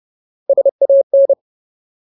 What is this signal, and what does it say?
Morse code, heard as a keyed tone, about 30 wpm, 550 Hz. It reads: SAN
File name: Ultimate Word List / 30WPM Ultimate